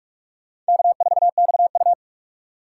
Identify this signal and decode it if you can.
Morse code, heard as a keyed tone, about 59 wpm, spaced as for 52 wpm, 700 Hz. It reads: K4XU
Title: Speed Racing - USA 1x2 Call Signs 45wpm